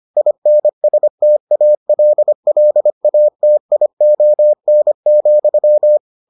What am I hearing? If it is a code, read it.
Morse code, heard as a keyed tone, about 25 wpm, 600 Hz. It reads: INSTALLATION,